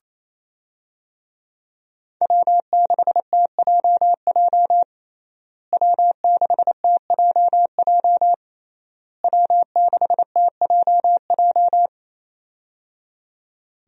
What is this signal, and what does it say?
Morse code, heard as a keyed tone, about 28 wpm, 700 Hz. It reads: W6TJJ W6TJJ W6TJJ
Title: USA Call Signs - 5 Characters - Repeated 3x - 28wpm